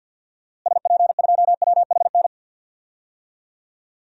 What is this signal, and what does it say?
Morse code, heard as a keyed tone, about 53 wpm, 700 Hz. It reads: SP2WHN
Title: DXCC-100 Call Signs - 6 Characters - Repeated 2x - 50wpm